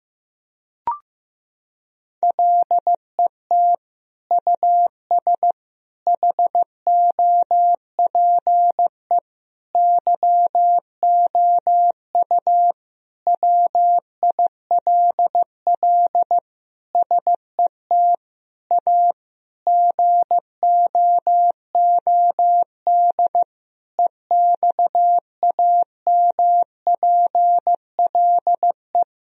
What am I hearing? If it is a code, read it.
Morse code, heard as a keyed tone, about 15 wpm, 700 Hz. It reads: LET US HOPE YOU WILL SET A GOOD EXAMPLE
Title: Sentences from Top 1000 Words - Review Part 1 - 15wpm